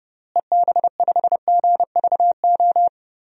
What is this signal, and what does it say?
Morse code, heard as a keyed tone, about 30 wpm, 700 Hz. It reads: EB5GVO
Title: DXCC-100 Call Signs - 6 Characters - Repeated 3x - 30wpm